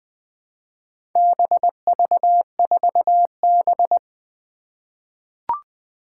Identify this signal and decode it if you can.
Morse code, heard as a keyed tone, about 20 wpm, 700 Hz. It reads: BV4B
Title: Four Challenging Letter-Number Combinations 20wpm